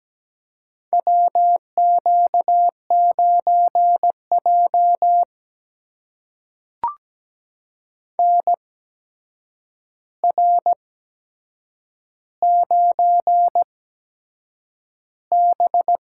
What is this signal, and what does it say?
Morse code, heard as a keyed tone, about 17 wpm, spaced as for 7 wpm, 700 Hz. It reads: WQ9J NR9B
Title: USA Call Signs - 4 Characters - 8x Character Spacing - 17wpm